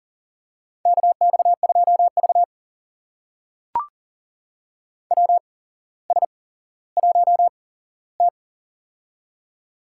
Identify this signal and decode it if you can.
Morse code, heard as a keyed tone, about 37 wpm, spaced as for 17 wpm, 700 Hz. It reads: KX2V WS1T
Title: USA Call Signs - 4 Characters - 8x Character Spacing - 40wpm